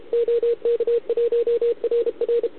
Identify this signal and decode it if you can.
Morse code, heard as a keyed tone, about 31 wpm, 470 Hz, with radio noise. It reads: OK1RR